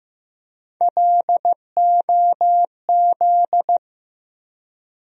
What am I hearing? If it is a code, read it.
Morse code, heard as a keyed tone, about 15 wpm, 700 Hz. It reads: LOZ
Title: USA Call Sign Postfixes - 3 Letters 15wpm